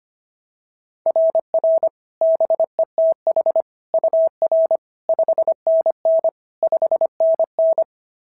Morse code, heard as a keyed tone, about 22 wpm, 650 Hz. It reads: RR BETH UR 5NN 5NN
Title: SOTA QSO Practice 25wpm